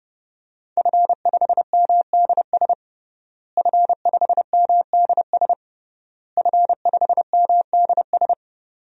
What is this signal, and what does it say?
Morse code, heard as a keyed tone, about 30 wpm, 700 Hz. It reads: F5MDS F5MDS F5MDS
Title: DXCC-100 Call Signs - 5 Characters - Repeated 3x - 30wpm